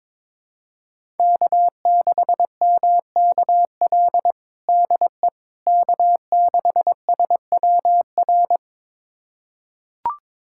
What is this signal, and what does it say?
Morse code, heard as a keyed tone, about 22 wpm, 700 Hz. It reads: K6MKL DE K6SWR
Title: USA Callsign DE Callsign - 5 Characters - 22wpm